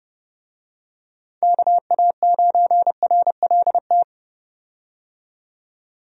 Morse code, heard as a keyed tone, about 30 wpm, 700 Hz. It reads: KA9RLT
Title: USA Call Signs - 6 Characters - 30wpm